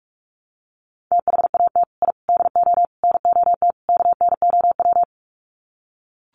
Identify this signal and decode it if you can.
Morse code, heard as a keyed tone, about 44 wpm, 690 Hz. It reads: THAT I DO NOT KNOW
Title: Sets of 5 Words Encore2 45wpm